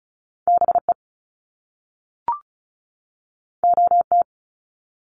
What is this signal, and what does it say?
Morse code, heard as a keyed tone, about 36 wpm, 700 Hz. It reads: BE OT
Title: Top Two Letter Combinations in Words 35wpm